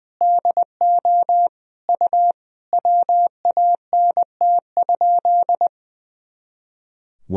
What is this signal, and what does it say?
Morse code, heard as a keyed tone, about 21 wpm, 700 Hz. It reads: DO U WANT?
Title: Sentences from Top 200 Words and Abbreviations 20wpm